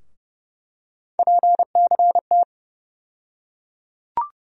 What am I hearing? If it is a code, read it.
Morse code, heard as a keyed tone, about 30 wpm, 700 Hz. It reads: PCT
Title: USA Call Sign Postfixes - 3 Letters 30wpm